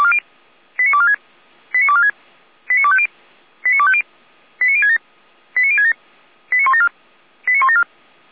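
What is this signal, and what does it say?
Five-tone selective call (ZVEI), heard as quick runs of five tones, tones of about 70 ms. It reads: 89246 89246 89250 89250 89067 89067 89174 89174
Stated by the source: ZVEI five-tone selective call received by SDR, archived by Feuerwehr Orlamünde fire brigade.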